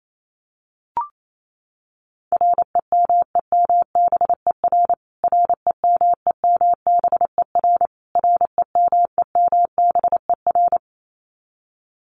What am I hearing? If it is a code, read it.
Morse code, heard as a keyed tone, about 28 wpm, 700 Hz. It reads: REMEMBER REMEMBER REMEMBER
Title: Top 500 Words Repeated 28wpm